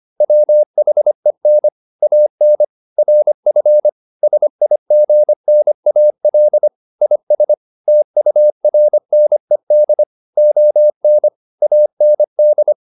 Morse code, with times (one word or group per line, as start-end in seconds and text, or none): 0.2-1.7 WHEN
2.0-2.6 AN
3.0-3.9 RF
4.2-6.7 SIGNAL
7.0-7.5 IS
7.9-10.0 TURNED
10.4-11.3 ON
11.6-12.7 AND